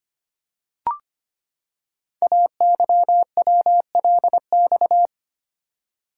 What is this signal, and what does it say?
Morse code, heard as a keyed tone, about 25 wpm, 700 Hz. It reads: AYWLX
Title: Code Groups - 5 Letters - 25wpm